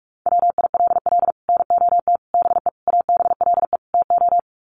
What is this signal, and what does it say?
Morse code, heard as a keyed tone, about 44 wpm, 700 Hz. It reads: WILL NOT BE ABLE TO
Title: Sets of 5 Words 45wpm